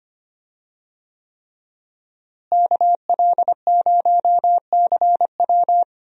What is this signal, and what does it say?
Morse code, heard as a keyed tone, about 25 wpm, 700 Hz. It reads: KL0CW
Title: USA Call Signs - 5 Characters - Repeated 3x - 25wpm